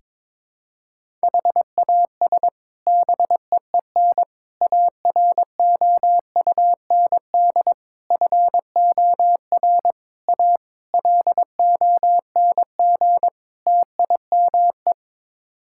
Morse code, heard as a keyed tone, about 22 wpm, 700 Hz. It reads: HAS BEEN AROUND FOR A LONG TIME
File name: practice / 22wpm / Sets of 7 Words - 22wpm